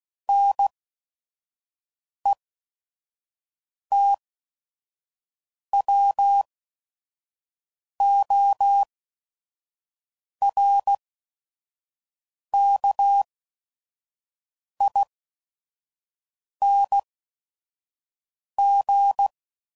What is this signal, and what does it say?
Morse code, heard as a keyed tone, about 16 wpm, spaced as for 5 wpm, 800 Hz. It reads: NETWORKING
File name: Suffixes / Special ING's / VerySlowToFaster ING's